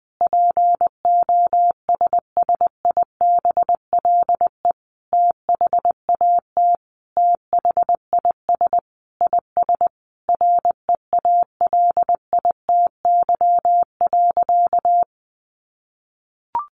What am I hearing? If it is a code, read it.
Morse code, heard as a keyed tone, about 20 wpm, 700 Hz. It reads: POSSIBLE THAT THIS IS REALITY.
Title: Sentences from Top 1200 Words 20wpm